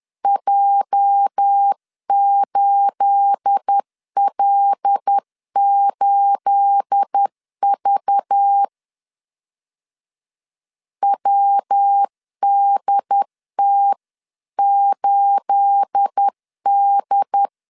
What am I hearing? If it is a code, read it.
Morse code, heard as a keyed tone, about 10 wpm, 800 Hz. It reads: J8L8V WDT8D
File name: LEZIONI / 9 28 10wpm 2842